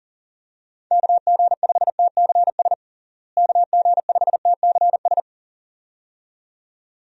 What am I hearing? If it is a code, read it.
Morse code, heard as a keyed tone, about 37 wpm, 700 Hz. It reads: KG5TCS KG5TCS
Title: USA Call Signs - 6 Characters - Repeated 2x - 40wpm